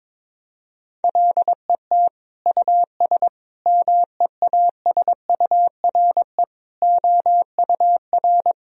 Morse code, heard as a keyed tone, about 22 wpm, 700 Hz. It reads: LET US MEASURE OUR
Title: Sentences from Top 1000 Words - Review Part 2 - 22wpm